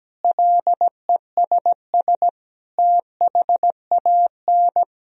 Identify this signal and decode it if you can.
Morse code, heard as a keyed tone, about 17 wpm, 700 Hz. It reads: LESS THAN
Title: Sentences from Top 1000 Words - Review Part 2 - 17wpm